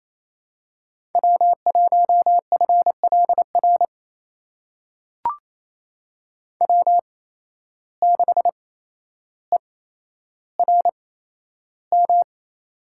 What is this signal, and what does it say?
Morse code, heard as a keyed tone, about 28 wpm, spaced as for 12 wpm, 700 Hz. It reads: W1FLR W6ERM
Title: USA Call Signs - 5 Characters - 8x Character Spacing - 28wpm